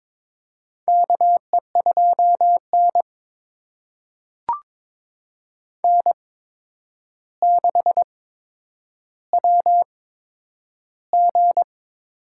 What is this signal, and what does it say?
Morse code, heard as a keyed tone, about 22 wpm, spaced as for 9 wpm, 700 Hz. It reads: KE2N N6WG
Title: USA Call Signs - 4 Characters - 8x Character Spacing - 22wpm